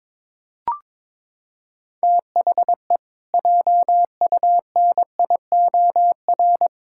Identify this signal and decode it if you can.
Morse code, heard as a keyed tone, about 22 wpm, 700 Hz. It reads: THE JUNIOR
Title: Pangram Challenge v2 - 22wpm